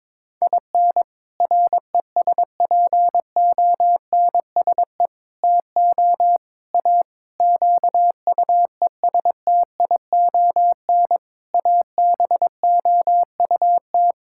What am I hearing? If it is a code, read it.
Morse code, heard as a keyed tone, about 22 wpm, 700 Hz. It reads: IN RESPONSE TO A QUESTION ABOUT
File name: practice / 22wpm / Sets of 6 Words - 22wpm